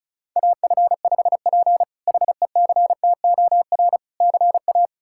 Morse code, heard as a keyed tone, about 34 wpm, 700 Hz. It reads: AF5P HECTOR CA
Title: K1USN SST QSO Practice - 35wpm